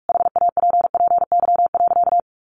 Morse code, heard as a keyed tone, about 44 wpm, 700 Hz. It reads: HAPPY.
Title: Sentences from Top 1400 Words 45wpm